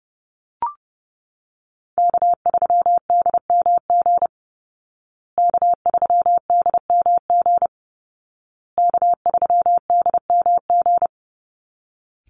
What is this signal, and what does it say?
Morse code, heard as a keyed tone, about 30 wpm, 700 Hz. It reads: K3DMG K3DMG K3DMG
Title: Call Signs Repeated 2x - 30wpm